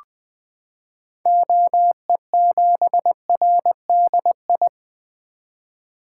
Morse code, heard as a keyed tone, about 20 wpm, 700 Hz. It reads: OE7RDI
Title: 2023 SOTA Activator Call Signs - 20wpm